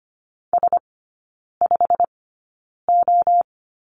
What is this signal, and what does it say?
Morse code, heard as a keyed tone, about 25 wpm, spaced as for 9 wpm, 690 Hz. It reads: S5O
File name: Lesson 15e - Number 5 - Three Characters 25wpm